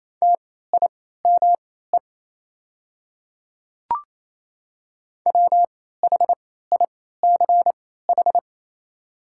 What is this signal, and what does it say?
Morse code, heard as a keyed tone, about 28 wpm, spaced as for 16 wpm, 700 Hz. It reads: TIME WHICH
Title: 3x Character Spacing - Top 200 Words - 28wpm